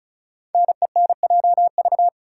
Morse code, heard as a keyed tone, about 36 wpm, 690 Hz. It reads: NENJV
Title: Code Groups - 5 Letters - 35wpm